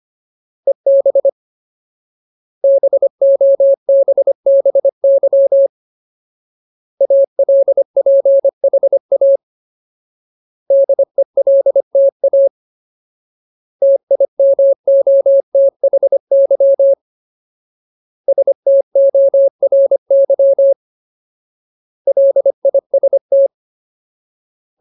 Morse code, heard as a keyed tone, about 25 wpm, 550 Hz. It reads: EB BOBBY ALPHA DELTA TIMOTHY STORY LIST
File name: Ultimate Word List / 25WPM Ultimate